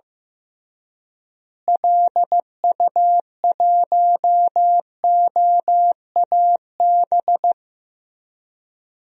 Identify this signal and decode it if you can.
Morse code, heard as a keyed tone, about 15 wpm, 700 Hz. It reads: LU1OAB
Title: DXCC-100 Call Signs - 6 Characters - Repeated 3x - 15wpm